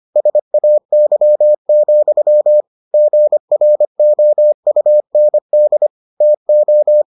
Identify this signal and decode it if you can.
Morse code, heard as a keyed tone, about 25 wpm, 600 Hz. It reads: SAY, GROUND TO